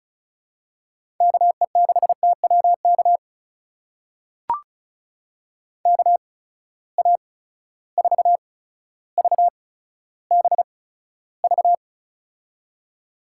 Morse code, heard as a keyed tone, about 34 wpm, spaced as for 15 wpm, 700 Hz. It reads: KE6TWK KA4VBV
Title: USA Call Signs - 8x Character Spacing - 35wpm